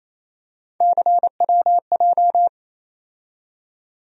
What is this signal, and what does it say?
Morse code, heard as a keyed tone, about 28 wpm, 700 Hz. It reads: CWJ
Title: USA Call Sign Postfixes - 3 Letters 28wpm